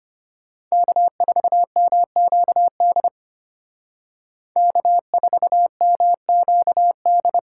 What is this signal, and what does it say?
Morse code, heard as a keyed tone, about 31 wpm, 700 Hz. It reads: K4MQD K4MQD
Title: Speed Racing - USA 1x3 Call Signs 25wpm